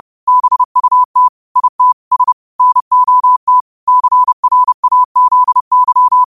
Morse code, heard as a keyed tone, about 30 wpm, 1000 Hz. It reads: BUT IT S NOT CRAZY